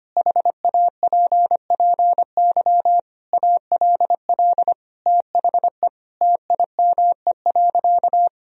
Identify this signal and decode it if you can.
Morse code, heard as a keyed tone, about 25 wpm, 700 Hz. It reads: HAPPY ALL THE TIME.
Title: Sentences from Top 1000 Words - Review Part 4 - 25wpm